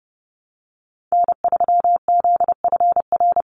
Morse code, heard as a keyed tone, about 30 wpm, 700 Hz. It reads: N3ZFR
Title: USA Call Signs 30wpm